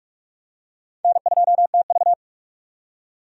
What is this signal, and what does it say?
Morse code, heard as a keyed tone, about 46 wpm, 700 Hz. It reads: N2TV
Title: USA Call Signs - 4 Characters - 45wpm